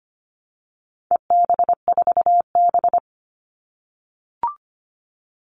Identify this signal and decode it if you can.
Morse code, heard as a keyed tone, about 25 wpm, 700 Hz. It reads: EB4B
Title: All Cut Numbers - 4-Digit 25wpm